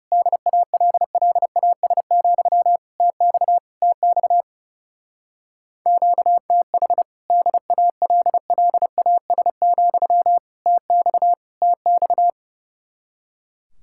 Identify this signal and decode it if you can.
Morse code, heard as a keyed tone, about 36 wpm, 700 Hz. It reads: DALLAS, TX TX QTH DALLAS, TX TX
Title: Speed-Racing Common QSO Phrases - 30wpm